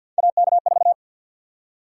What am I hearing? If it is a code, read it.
Morse code, heard as a keyed tone, about 50 wpm, 700 Hz. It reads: AK4
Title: USA Call Sign Prefixes - 50wpm